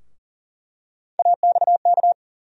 Morse code, heard as a keyed tone, about 40 wpm, 700 Hz. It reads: AXK
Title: USA Call Sign Postfixes - 3 Letters 40wpm